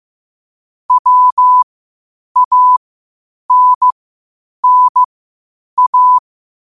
Morse code, heard as a keyed tone, about 15 wpm, spaced as for 8 wpm, 1000 Hz. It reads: WANNA
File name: qotdfiles / 8wpm